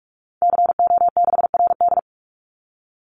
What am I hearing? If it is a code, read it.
Morse code, heard as a keyed tone, about 44 wpm, 700 Hz. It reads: CO6RD
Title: Call Signs - 45wpm